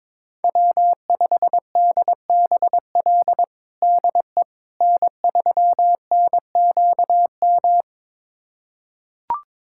Morse code, heard as a keyed tone, about 22 wpm, 700 Hz. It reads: W5DBL DE N3NQM
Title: USA Callsign DE Callsign - 5 Characters - 22wpm